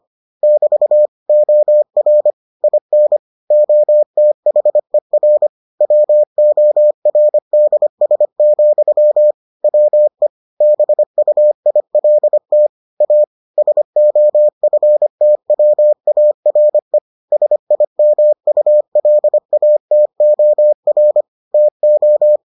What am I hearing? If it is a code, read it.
Morse code, heard as a keyed tone, about 25 wpm, 600 Hz. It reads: <BT> OR IN OTHER WORDS, WE BUILT A SOFTWARE SIMULATOR TO